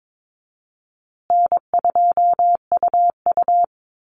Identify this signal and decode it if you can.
Morse code, heard as a keyed tone, about 22 wpm, 700 Hz. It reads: N2UU
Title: Call Signs Repeated 3x - 22wpm